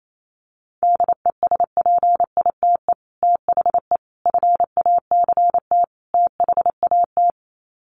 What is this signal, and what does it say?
Morse code, heard as a keyed tone, about 28 wpm, 700 Hz. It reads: DESPITE THE FACT THAT
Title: Sets of 4 Words 28wpm